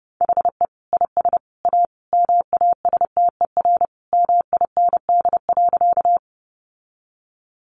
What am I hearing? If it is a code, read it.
Morse code, heard as a keyed tone, about 30 wpm, 700 Hz. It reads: HE IS A MASTER MIND.
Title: Sentences from Top 1300 Words 30wpm